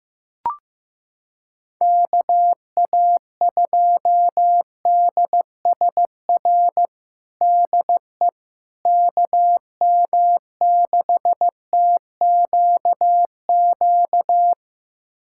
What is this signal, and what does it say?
Morse code, heard as a keyed tone, about 15 wpm, 700 Hz. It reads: KA2DSR DE KM6TQQ
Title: USA Callsign DE Callsign 15wpm